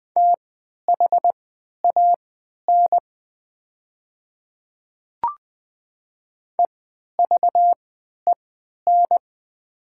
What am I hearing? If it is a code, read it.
Morse code, heard as a keyed tone, about 20 wpm, spaced as for 11 wpm, 700 Hz. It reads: THAN EVEN
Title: 3x Character Spacing - Top 200 Words - 20wpm